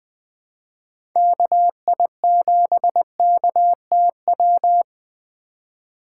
Morse code, heard as a keyed tone, about 20 wpm, 700 Hz. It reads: KI7KTW
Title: Speed Racing - USA 2x3 Call Signs 17wpm